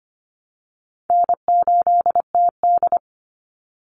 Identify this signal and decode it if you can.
Morse code, heard as a keyed tone, about 25 wpm, 700 Hz. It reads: N8TD
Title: Call Signs Repeated 3x - 25wpm